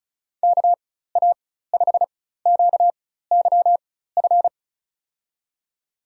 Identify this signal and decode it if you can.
Morse code, heard as a keyed tone, about 36 wpm, spaced as for 16 wpm, 700 Hz. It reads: KA5QYF
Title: USA Call Signs - 6 Characters - 4x Character Spacing - 35wpm